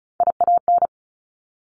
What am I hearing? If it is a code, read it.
Morse code, heard as a keyed tone, about 36 wpm, 700 Hz. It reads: IAN
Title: Top Three Letter Combinations in Words 35wpm